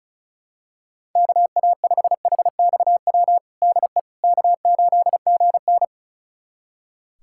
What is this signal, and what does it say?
Morse code, heard as a keyed tone, about 36 wpm, 700 Hz. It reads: KA5HXW DE K8GN
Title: USA Callsign DE Callsign 35wpm